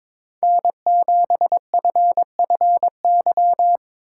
Morse code, heard as a keyed tone, about 22 wpm, 700 Hz. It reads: N7FFY
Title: USA Call Signs - 5 Characters - 3x Character Spacing - 22wpm